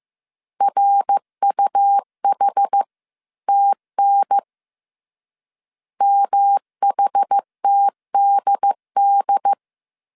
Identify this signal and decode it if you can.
Morse code, heard as a keyed tone, about 15 wpm, 800 Hz. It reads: RUHTN MHTDD